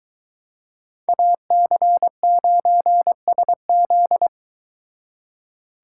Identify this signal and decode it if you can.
Morse code, heard as a keyed tone, about 23 wpm, 700 Hz. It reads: AC9SZ